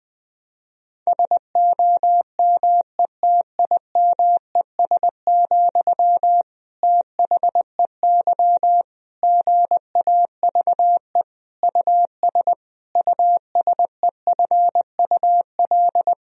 Morse code, heard as a keyed tone, about 21 wpm, 700 Hz. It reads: SOMETIMES, THEY GAVE US USEFUL
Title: Sentences from Top 1000 Words - Review Part 4 - 20wpm